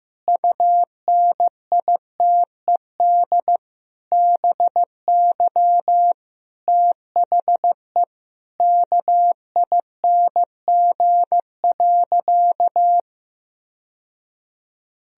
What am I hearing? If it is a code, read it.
Morse code, heard as a keyed tone, about 15 wpm, 700 Hz. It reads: UNITED BY THE KING.
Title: Sentences from Top 1000 Words - Review Part 4 - 15wpm